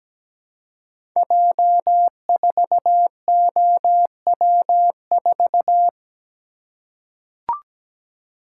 Morse code, heard as a keyed tone, about 17 wpm, 700 Hz. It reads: J4OW4
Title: Code Groups - 5 Letter:Numbers - 17wpm